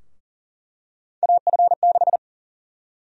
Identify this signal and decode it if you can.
Morse code, heard as a keyed tone, about 40 wpm, 700 Hz. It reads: AF6
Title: USA Call Sign Prefixes - Repeated 2x - 40wpm